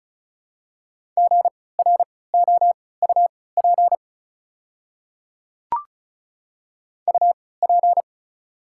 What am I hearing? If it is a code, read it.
Morse code, heard as a keyed tone, about 34 wpm, spaced as for 20 wpm, 700 Hz. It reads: GROUP UP
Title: 3x Character Spacing - Top 500 Words - 35wpm